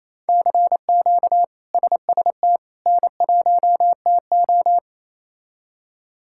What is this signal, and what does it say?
Morse code, heard as a keyed tone, about 28 wpm, 700 Hz. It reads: CQ SST N1TO
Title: K1USN SST QSO Practice - 28wpm